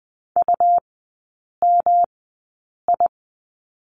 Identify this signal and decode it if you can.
Morse code, heard as a keyed tone, about 20 wpm, spaced as for 8 wpm, 700 Hz. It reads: UMI
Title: 184 Lesson 30e - Letter b - Three Characters 20wpm